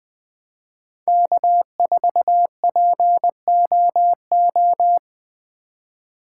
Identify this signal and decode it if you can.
Morse code, heard as a keyed tone, about 20 wpm, 700 Hz. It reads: K4POO
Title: USA Call Signs - 5 Characters - 2x Character Spacing - 20wpm